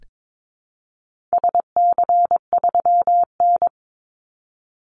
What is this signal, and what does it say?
Morse code, heard as a keyed tone, about 22 wpm, 700 Hz. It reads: SC3N
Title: Call Signs - 22wpm